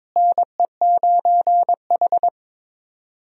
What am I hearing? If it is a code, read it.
Morse code, heard as a keyed tone, about 22 wpm, 700 Hz. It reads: NE9H
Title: USA Call Signs - 4 Characters - Repeated 3x - 22wpm